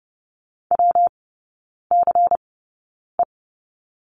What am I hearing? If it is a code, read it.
Morse code, heard as a keyed tone, about 30 wpm, spaced as for 9 wpm, 700 Hz. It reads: WCE
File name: Lesson 19e - Letter w - Three Characters 30wpm